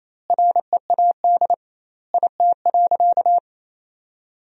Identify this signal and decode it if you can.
Morse code, heard as a keyed tone, about 28 wpm, 700 Hz. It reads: READ IT.
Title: 2x Word-Spacing - Sentences from Top 500 Words - 28wpm